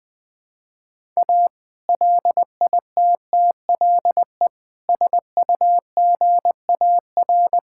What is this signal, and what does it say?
Morse code, heard as a keyed tone, about 21 wpm, 700 Hz. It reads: A LITTLE SUGAR
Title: Sentences with 2-Word Phrases - 20wpm